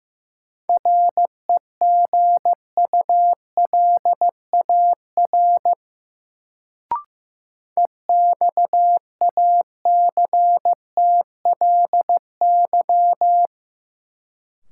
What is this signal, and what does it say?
Morse code, heard as a keyed tone, about 15 wpm, 700 Hz. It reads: REGULAR EXACTLY
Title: Sentences from Top 900 Words - New Word Drill - 15wpm